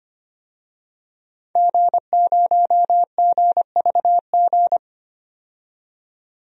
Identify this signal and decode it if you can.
Morse code, heard as a keyed tone, about 25 wpm, 700 Hz. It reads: G0GVG
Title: DXCC-100 Call Signs - 5 Characters - 25wpm